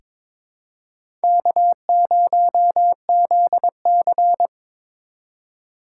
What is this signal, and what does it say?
Morse code, heard as a keyed tone, about 22 wpm, 700 Hz. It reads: K0ZC
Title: USA Call Signs - 4 Characters - 3x Character Spacing - 22wpm